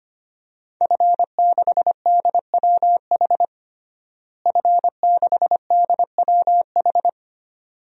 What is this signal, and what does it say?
Morse code, heard as a keyed tone, about 25 wpm, 700 Hz. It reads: F6DWH F6DWH
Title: DXCC-100 Call Signs - 5 Characters - Repeated 2x - 25wpm